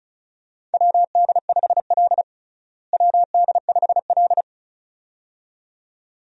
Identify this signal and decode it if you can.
Morse code, heard as a keyed tone, about 36 wpm, 700 Hz. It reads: WD5L WD5L
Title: USA Call Signs - 4 Characters - Repeated 2x - 35wpm